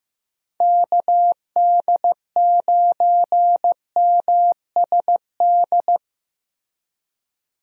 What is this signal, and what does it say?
Morse code, heard as a keyed tone, about 15 wpm, 700 Hz. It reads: KD9MSD
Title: USA Call Signs - 6 Characters - Repeated 2x - 15wpm